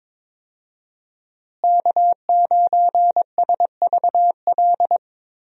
Morse code, heard as a keyed tone, about 22 wpm, 700 Hz. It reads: K9SVL
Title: USA Call Signs - 5 Characters - Repeated 3x - 22wpm